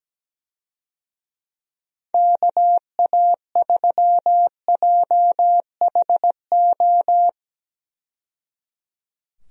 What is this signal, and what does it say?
Morse code, heard as a keyed tone, about 17 wpm, 700 Hz. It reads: KA3JHO
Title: USA Call Signs - 6 Characters - 17wpm